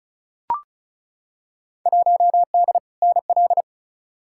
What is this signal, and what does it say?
Morse code, heard as a keyed tone, about 36 wpm, 700 Hz. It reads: 1D NL
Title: Field Day Exchanges Encore - 35wpm